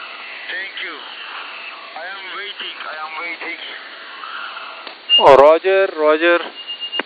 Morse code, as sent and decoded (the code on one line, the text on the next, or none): none